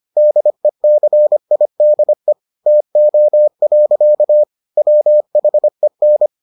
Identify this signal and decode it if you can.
Morse code, heard as a keyed tone, about 25 wpm, 600 Hz. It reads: DECIDE TO. WHEN